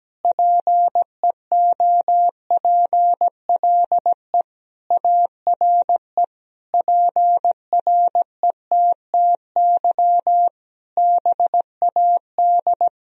Morse code, heard as a keyed tone, about 17 wpm, 700 Hz. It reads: PEOPLE ARE PRETTY BAD
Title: Sentences from Top 1000 Words - Review Part 3 - 17wpm